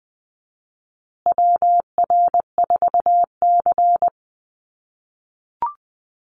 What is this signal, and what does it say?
Morse code, heard as a keyed tone, about 20 wpm, 700 Hz. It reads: WR4C
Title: USA Call Signs Repeated 3x 20wpm